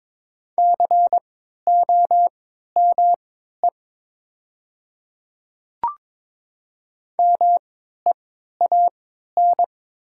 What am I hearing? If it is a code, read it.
Morse code, heard as a keyed tone, about 22 wpm, spaced as for 12 wpm, 700 Hz. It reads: COME MEAN
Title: 3x Character Spacing - Top 200 Words - 22wpm